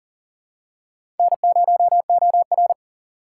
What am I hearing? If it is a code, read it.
Morse code, heard as a keyed tone, about 40 wpm, 700 Hz. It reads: N0OR
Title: USA Call Signs - 2x Character Spacing - 40wpm